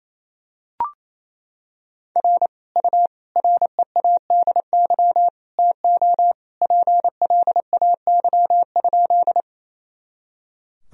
Morse code, heard as a keyed tone, about 28 wpm, 700 Hz. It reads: R U READY TO PLAY?
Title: Sentences from Top 400 Words and Abbreviations 28wpm